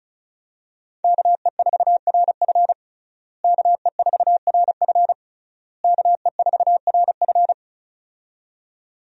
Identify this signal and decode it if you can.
Morse code, heard as a keyed tone, about 35 wpm, 700 Hz. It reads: KE4RF KE4RF KE4RF
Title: USA Call Signs - 5 Characters - Repeated 3x - 35wpm